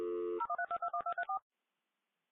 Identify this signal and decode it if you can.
DTMF, heard as telephone touch tones, tones of about 90 ms. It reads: 013221234